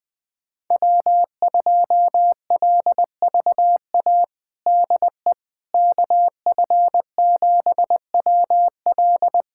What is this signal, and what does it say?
Morse code, heard as a keyed tone, about 20 wpm, 700 Hz. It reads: W2LVA DE KF7WL
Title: USA Callsign DE Callsign - 5 Characters - 20wpm